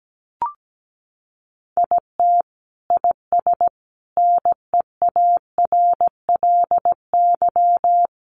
Morse code, heard as a keyed tone, about 17 wpm, 700 Hz. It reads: IT IS NEARLY
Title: Sentences from Top 1500 Words 17wpm